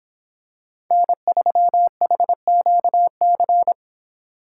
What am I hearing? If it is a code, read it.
Morse code, heard as a keyed tone, about 25 wpm, 700 Hz. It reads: N3HQC